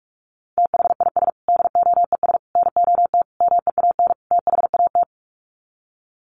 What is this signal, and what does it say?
Morse code, heard as a keyed tone, about 44 wpm, 700 Hz. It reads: THIS DOES NOT MEAN THAT
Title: Sets of 5 Words Encore2 45wpm